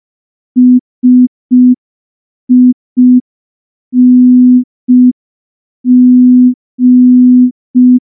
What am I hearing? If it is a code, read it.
Morse code, heard as a keyed tone, about 5 wpm, 250 Hz. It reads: SING
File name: morse / morse-training-05WPM